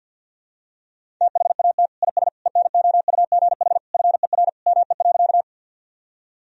Morse code, heard as a keyed tone, about 51 wpm, 700 Hz. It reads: THAT IS ENOUGH FER ME.
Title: Sentences from Top 400 Words and Abbreviations 50wpm